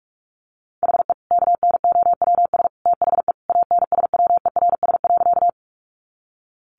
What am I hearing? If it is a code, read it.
Morse code, heard as a keyed tone, about 44 wpm, 700 Hz. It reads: HE KNOWS THE ANSWERS.
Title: Additional Easy Phrases 45wpm